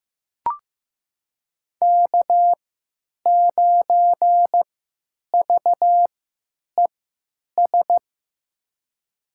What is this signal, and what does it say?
Morse code, heard as a keyed tone, about 15 wpm, spaced as for 9 wpm, 700 Hz. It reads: K9VES
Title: USA Call Signs - 5 Characters - 3x Character Spacing - 15wpm